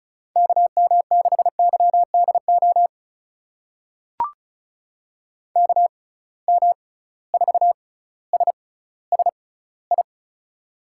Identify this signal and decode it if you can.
Morse code, heard as a keyed tone, about 34 wpm, spaced as for 18 wpm, 700 Hz. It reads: KM6YDO KM4SSI
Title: USA Call Signs - 6 Characters - 6x Character Spacing - 35wpm